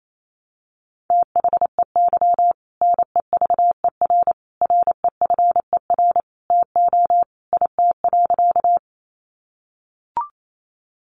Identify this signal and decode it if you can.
Morse code, heard as a keyed tone, about 28 wpm, 700 Hz. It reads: THEY NEVER REFER TO IT.
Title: Sentences from Top 1000 Words 28wpm